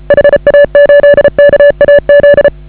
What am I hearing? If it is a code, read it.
Morse code, heard as a keyed tone, about 34 wpm, 560 Hz. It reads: HA8KAZ